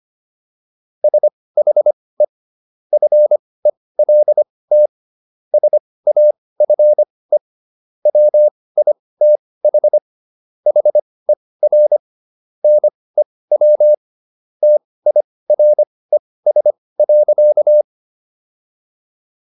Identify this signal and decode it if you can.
Morse code, heard as a keyed tone, about 25 wpm, spaced as for 18 wpm, 600 Hz. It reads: SHE FELT SAFE WITH HER NEW TIRES.